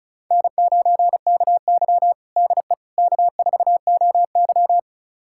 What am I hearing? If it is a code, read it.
Morse code, heard as a keyed tone, about 36 wpm, 700 Hz. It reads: N9KY DE K4OY
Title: USA Callsign DE Callsign - 4 Characters - 35wpm